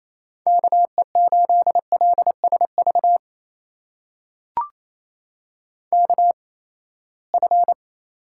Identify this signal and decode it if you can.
Morse code, heard as a keyed tone, about 28 wpm, spaced as for 19 wpm, 700 Hz. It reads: KE8LSV KF
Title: USA Call Signs - 6 Characters - 8x Character Spacing - 28wpm